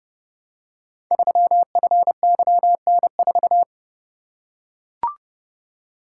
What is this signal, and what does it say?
Morse code, heard as a keyed tone, about 30 wpm, 700 Hz. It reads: 3FYN4
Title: Code Groups - 5 Letter:Numbers - 30wpm